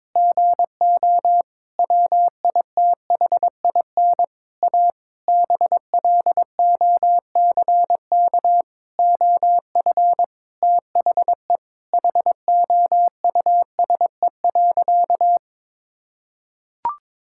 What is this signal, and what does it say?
Morse code, heard as a keyed tone, about 22 wpm, 700 Hz. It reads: GO WITHIN A BLOCK OF THE HOUSE.
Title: Sentences from Top 1600 Words 22wpm